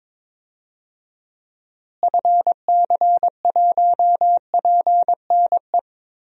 Morse code, heard as a keyed tone, about 22 wpm, 700 Hz. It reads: FC1PNE
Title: DXCC-100 Call Signs - 6 Characters - Repeated 2x - 22wpm